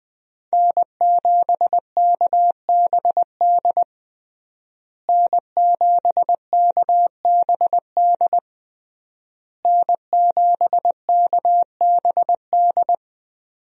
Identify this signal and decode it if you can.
Morse code, heard as a keyed tone, about 20 wpm, 700 Hz. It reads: N7KBD N7KBD N7KBD
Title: USA Call Signs - 5 Characters - Repeated 3x - 20wpm